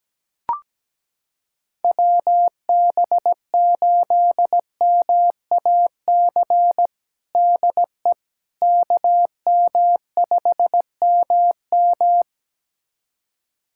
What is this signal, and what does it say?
Morse code, heard as a keyed tone, about 17 wpm, 700 Hz. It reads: WB8MAC DE KM5MM
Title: USA Callsign DE Callsign 17wpm